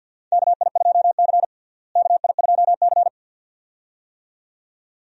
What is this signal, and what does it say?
Morse code, heard as a keyed tone, about 49 wpm, 700 Hz. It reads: KI2C KI2C
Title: USA Call Signs - 4 Characters - Repeated 2x - 50wpm